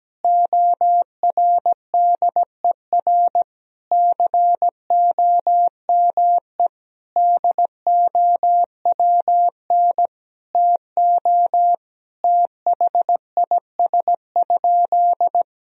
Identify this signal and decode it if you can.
Morse code, heard as a keyed tone, about 17 wpm, 700 Hz. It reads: ORDER COME DOWN TO THIS?
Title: Sentences from Top 1000 Words - Review Part 3 - 17wpm